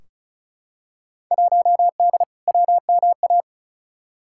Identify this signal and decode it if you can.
Morse code, heard as a keyed tone, about 36 wpm, 700 Hz. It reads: 1D WMA
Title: Field Day Exchanges - 35wpm